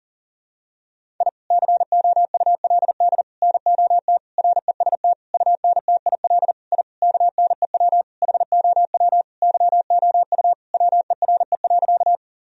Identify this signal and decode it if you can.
Morse code, heard as a keyed tone, about 37 wpm, 700 Hz. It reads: I COULD NOT REST UNTIL I KNEW HOW YOU WERE.
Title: Sentences from Top 1000 Words - Review Part 1 - 40wpm